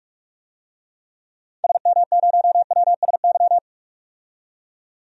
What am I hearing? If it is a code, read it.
Morse code, heard as a keyed tone, about 46 wpm, 700 Hz. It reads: SM0WSY
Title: DXCC-100 Call Signs - 6 Characters - 45wpm